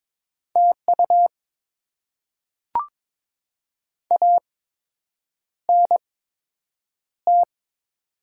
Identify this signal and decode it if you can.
Morse code, heard as a keyed tone, about 22 wpm, spaced as for 8 wpm, 700 Hz. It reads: TU ANT
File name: Top 20 QSO Elements - 8x Character Spacing - 22wpm